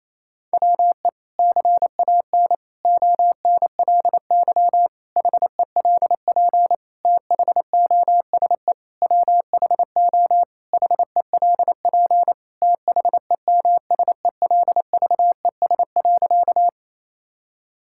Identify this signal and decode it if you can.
Morse code, heard as a keyed tone, about 27 wpm, 700 Hz. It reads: WE CAN ONLY HELP THOSE WHO HELP THEMSELVES.
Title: Sentences from Top 1000 Words - Review Part 5 - 28wpm